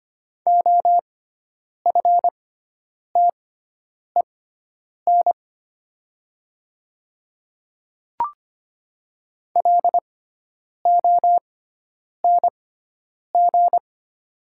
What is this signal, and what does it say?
Morse code, heard as a keyed tone, about 25 wpm, spaced as for 9 wpm, 700 Hz. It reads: OFTEN LONG
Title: 6x Character Spacing - Top 500 Words - 25wpm